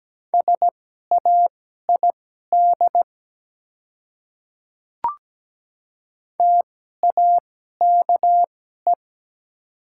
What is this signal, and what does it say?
Morse code, heard as a keyed tone, about 17 wpm, spaced as for 12 wpm, 700 Hz. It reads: SAID TAKE
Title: 2x Character Spacing - Top 200 Words - 17wpm